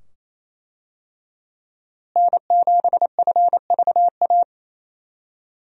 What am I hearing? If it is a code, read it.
Morse code, heard as a keyed tone, about 28 wpm, 700 Hz. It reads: N7FVA
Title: USA Call Signs - 5 Characters - Repeated 2x - 28wpm